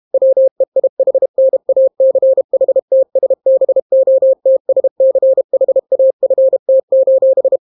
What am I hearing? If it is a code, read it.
Morse code, heard as a keyed tone, about 32 wpm, 530 Hz. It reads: WEIHNACHTSBOTSCHAFT: